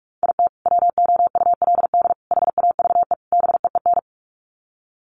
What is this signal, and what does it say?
Morse code, heard as a keyed tone, about 44 wpm, 700 Hz. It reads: IT WOULD HAVE BEEN
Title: Sets of 4 Words 45wpm